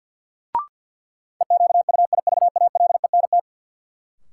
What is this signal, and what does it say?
Morse code, heard as a keyed tone, about 47 wpm, 700 Hz. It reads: EQUIVALENT
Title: Sentences from Top 1600 Words - New Word Drill - 50wpm